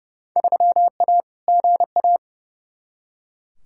Morse code, heard as a keyed tone, about 31 wpm, 700 Hz. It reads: 3A GA
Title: Field Day Exchanges - 30wpm